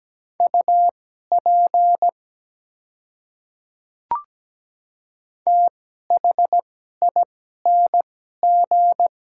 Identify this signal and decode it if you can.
Morse code, heard as a keyed tone, about 17 wpm, spaced as for 12 wpm, 700 Hz. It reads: UP THING
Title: 2x Character Spacing - Top 100 Words - 17wpm